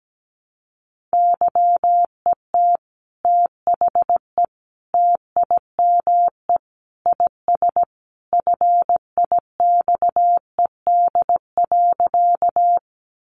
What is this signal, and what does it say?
Morse code, heard as a keyed tone, about 17 wpm, 700 Hz. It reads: YET THE TIME IS FIXED.
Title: Sentences from Top 1100 Words 17wpm